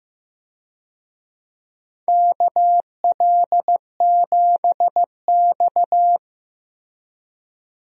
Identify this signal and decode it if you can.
Morse code, heard as a keyed tone, about 15 wpm, 700 Hz. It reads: KL7X
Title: DXCC-100 Call Signs - 4 Characters - 15wpm